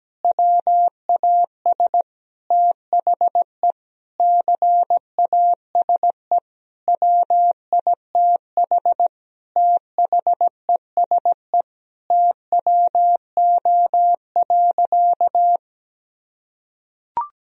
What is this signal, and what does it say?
Morse code, heard as a keyed tone, about 17 wpm, 700 Hz. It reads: WAS THE CASE WITH THESE TWO.
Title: Sentences from Top 1000 Words - Review Part 1 - 17wpm